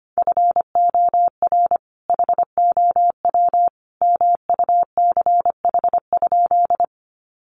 Morse code, heard as a keyed tone, about 25 wpm, 700 Hz. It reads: FOR HOW MUCH?
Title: Sentences from Top 200 Words 25wpm